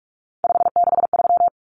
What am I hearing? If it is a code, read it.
Morse code, heard as a keyed tone, about 44 wpm, 700 Hz. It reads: 563
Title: Numbers - 3-Digit 45wpm